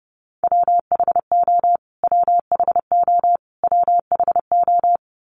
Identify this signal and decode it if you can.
Morse code, heard as a keyed tone, about 30 wpm, 700 Hz. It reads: WHO WHO WHO
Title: Top 300 Words Repeated 30wpm